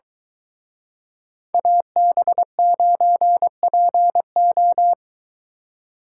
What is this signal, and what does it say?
Morse code, heard as a keyed tone, about 24 wpm, 700 Hz. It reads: AB9PO